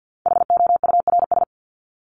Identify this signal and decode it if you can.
Morse code, heard as a keyed tone, about 50 wpm, 700 Hz. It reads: HOURS
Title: ES:S Suffix - Top 100 Words with Base Words 50wpm